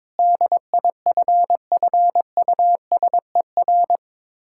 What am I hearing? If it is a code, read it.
Morse code, heard as a keyed tone, about 22 wpm, 690 Hz. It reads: DIFFUSER
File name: Difficult Words with Hints - 22wpm